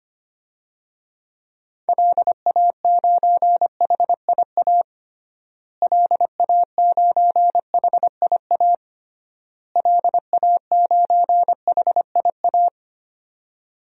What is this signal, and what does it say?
Morse code, heard as a keyed tone, about 25 wpm, 700 Hz. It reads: LA9HIA LA9HIA LA9HIA
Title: DXCC-100 Call Signs - 6 Characters - Repeated 3x - 25wpm